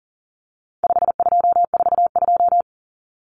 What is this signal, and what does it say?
Morse code, heard as a keyed tone, about 40 wpm, 700 Hz. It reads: 5242
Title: Numbers - 4-Digit 40wpm